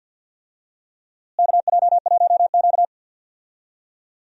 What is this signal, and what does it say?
Morse code, heard as a keyed tone, about 47 wpm, 700 Hz. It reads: KJ1Q
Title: USA Call Signs - 4 Characters - Repeated 2x - 50wpm